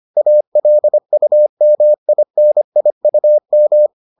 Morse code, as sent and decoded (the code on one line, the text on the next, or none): .- .-.. ..- -- .. -. .. ..- --
ALUMINIUM